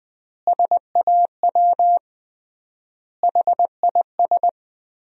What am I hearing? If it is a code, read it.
Morse code, heard as a keyed tone, about 20 wpm, 700 Hz. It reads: SAW HIS
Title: 3x Word-Spacing - Sentences from Top 500 Words - 20wpm